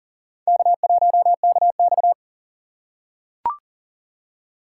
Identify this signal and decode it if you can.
Morse code, heard as a keyed tone, about 40 wpm, 700 Hz. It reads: K1KX
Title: USA Call Signs - 4 Characters - 2x Character Spacing - 40wpm